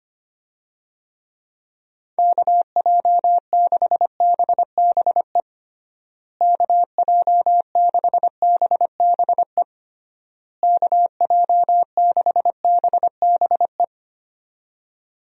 Morse code, heard as a keyed tone, about 25 wpm, 700 Hz. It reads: KJ6BBE KJ6BBE KJ6BBE
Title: USA Call Signs - 6 Characters - Repeated 3x - 25wpm